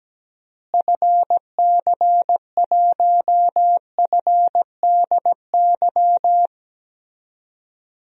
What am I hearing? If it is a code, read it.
Morse code, heard as a keyed tone, about 17 wpm, 700 Hz. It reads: FC1FDY
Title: DXCC-100 Call Signs - 6 Characters - Repeated 3x - 17wpm